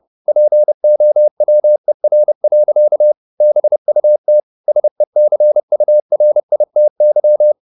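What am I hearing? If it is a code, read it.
Morse code, heard as a keyed tone, about 30 wpm, 600 Hz. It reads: POWER. BUT SECURITY
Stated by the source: Morse code at 29 wpm, sending Ken Follett's novel Never in English, from CW med Gnister.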